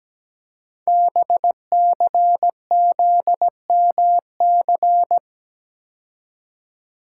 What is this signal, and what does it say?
Morse code, heard as a keyed tone, about 17 wpm, 700 Hz. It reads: BCZMC
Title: Code Groups - 5 Letters - 17wpm